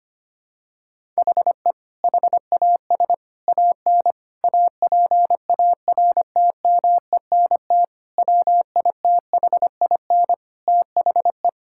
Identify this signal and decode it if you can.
Morse code, heard as a keyed tone, about 25 wpm, 700 Hz. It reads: HE HAS AN APARTMENT WITHIN THE
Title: Sentences from Top 1000 Words - Review Part 6 - 25wpm